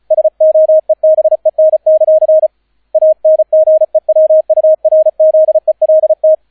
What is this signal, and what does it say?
Morse code, heard as a keyed tone, about 35 wpm, 630 Hz, with radio noise. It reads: SOEBEN; ANGEWURZELT